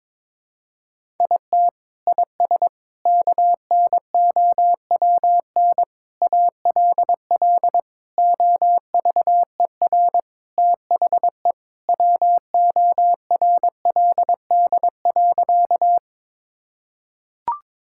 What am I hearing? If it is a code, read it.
Morse code, heard as a keyed tone, about 22 wpm, 700 Hz. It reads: IT IS KNOWN ALL OVER THE WORLD.
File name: practice / 22wpm / Sentences with 4-Word Phrases - 22wpm